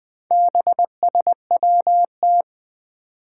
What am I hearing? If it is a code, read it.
Morse code, heard as a keyed tone, about 20 wpm, 700 Hz. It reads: BSWT
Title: Four Challenging Letter-Number Combinations 20wpm